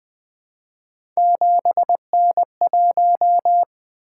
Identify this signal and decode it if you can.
Morse code, heard as a keyed tone, about 20 wpm, 700 Hz. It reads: 7N1
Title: Three Letter-Number Combinations - 20wpm